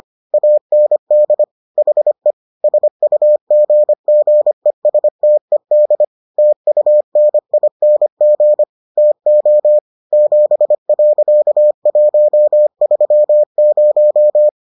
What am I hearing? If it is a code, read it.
Morse code, heard as a keyed tone, about 25 wpm, 600 Hz. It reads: AND HE SUGGESTED TUNING TO 7.130